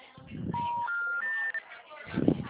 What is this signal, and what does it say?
Special information tone, with noise.